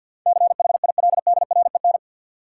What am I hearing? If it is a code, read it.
Morse code, heard as a keyed tone, about 51 wpm, 700 Hz. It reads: CHILDREN
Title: Sentences from Top 300 Words - New Word Drill - 50wpm